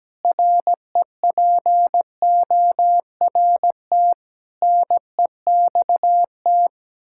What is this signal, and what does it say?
Morse code, heard as a keyed tone, about 17 wpm, 700 Hz. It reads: REPORT NEXT
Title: Sentences from Top 1000 Words - Review Part 6 - 17wpm